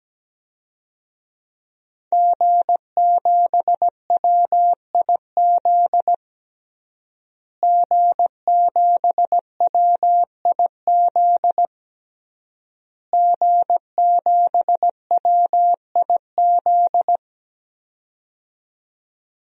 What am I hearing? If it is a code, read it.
Morse code, heard as a keyed tone, about 17 wpm, 700 Hz. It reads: G7WIZ G7WIZ G7WIZ
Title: DXCC-100 Call Signs - 5 Characters - Repeated 3x - 17wpm